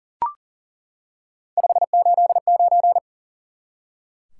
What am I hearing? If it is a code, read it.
Morse code, heard as a keyed tone, about 38 wpm, 700 Hz. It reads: 589
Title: Top 180 QSO Elements 40wpm